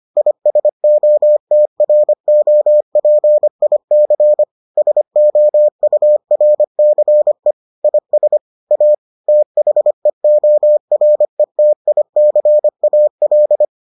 Morse code, heard as a keyed tone, about 25 wpm, 600 Hz. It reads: ISOTROPIC SOURCE IS A THEORETICAL